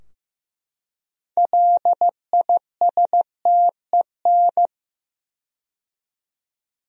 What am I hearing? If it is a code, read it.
Morse code, heard as a keyed tone, about 15 wpm, 700 Hz. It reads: LISTEN
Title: 4x Word-Spacing - Sentences from Top 400 Words - 15wpm